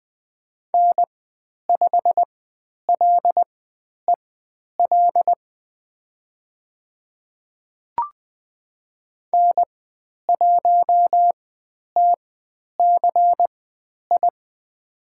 Morse code, heard as a keyed tone, about 20 wpm, spaced as for 10 wpm, 700 Hz. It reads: N5LEL N1TCI
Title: 095 Lesson 16g - Letter c - Call Signs 20wpm